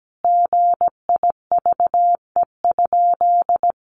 Morse code, heard as a keyed tone, about 17 wpm, 700 Hz. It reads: GIVE?
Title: Sentences from Top 1100 Words 17wpm